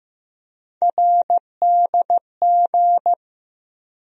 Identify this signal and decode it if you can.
Morse code, heard as a keyed tone, about 15 wpm, 700 Hz. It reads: RDG
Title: USA Call Sign Postfixes - 3 Letters 15wpm